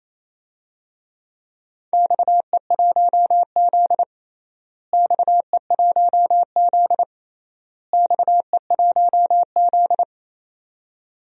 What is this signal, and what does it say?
Morse code, heard as a keyed tone, about 28 wpm, 700 Hz. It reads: XE1Z XE1Z XE1Z
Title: DXCC-100 Call Signs - 4 Characters - Repeated 3x - 28wpm